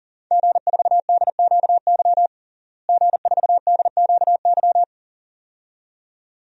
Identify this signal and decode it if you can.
Morse code, heard as a keyed tone, about 40 wpm, 700 Hz. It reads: G4DQY G4DQY
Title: DXCC-100 Call Signs - 5 Characters - Repeated 2x - 40wpm